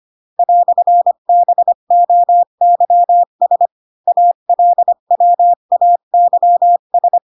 Morse code, heard as a keyed tone, about 25 wpm, 700 Hz. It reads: "BOYS ALWAYS
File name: Pinocchio / Pinocchio ch8 20 25 ik1hge